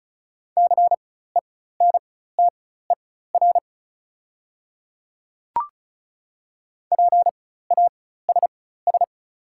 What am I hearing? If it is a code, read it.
Morse code, heard as a keyed tone, about 36 wpm, spaced as for 16 wpm, 700 Hz. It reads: CENTER PASS
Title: 4x Character Spacing - Top 500 Words - 35wpm